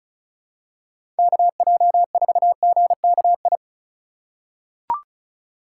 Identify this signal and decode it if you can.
Morse code, heard as a keyed tone, about 36 wpm, 700 Hz. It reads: KJ4GKI
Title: USA Call Signs - 6 Characters - 3x Character Spacing - 35wpm